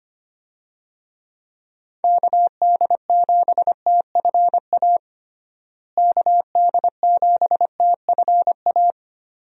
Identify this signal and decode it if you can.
Morse code, heard as a keyed tone, about 25 wpm, 700 Hz. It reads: KD7TFA KD7TFA
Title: USA Call Signs - 6 Characters - Repeated 3x - 25wpm